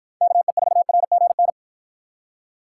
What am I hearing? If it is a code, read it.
Morse code, heard as a keyed tone, about 53 wpm, 700 Hz. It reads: KE4RGN